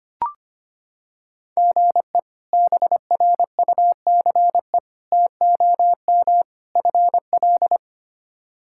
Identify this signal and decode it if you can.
Morse code, heard as a keyed tone, about 25 wpm, 700 Hz. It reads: GE BRUCE TOM FL
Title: K1USN SST QSO Practice - 25wpm